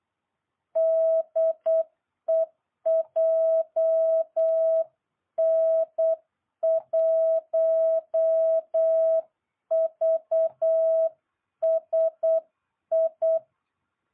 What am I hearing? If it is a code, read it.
Morse code, heard as a keyed tone, about 8 wpm, 660 Hz. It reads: DE JN1VSI